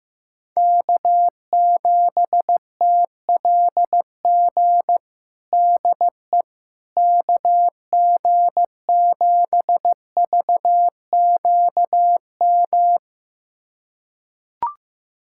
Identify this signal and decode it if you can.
Morse code, heard as a keyed tone, about 15 wpm, 700 Hz. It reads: K7TLG DE KG7VQM
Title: USA Callsign DE Callsign 15wpm